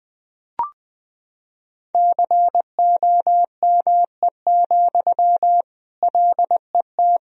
Morse code, heard as a keyed tone, about 20 wpm, 700 Hz. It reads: COME, LET
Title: Sentences from Top 1000 Words - Review Part 2 - 20wpm